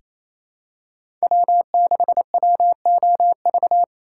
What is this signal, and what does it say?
Morse code, heard as a keyed tone, about 28 wpm, 700 Hz. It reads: W6WOV
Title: USA Call Signs - 5 Characters - 2x Character Spacing - 28wpm